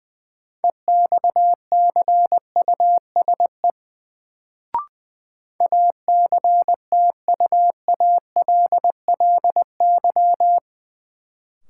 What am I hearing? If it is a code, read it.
Morse code, heard as a keyed tone, about 21 wpm, 700 Hz. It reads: EXCUSE ACTUALLY
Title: Sentences from Top 700 Words - New Word Drill - 20wpm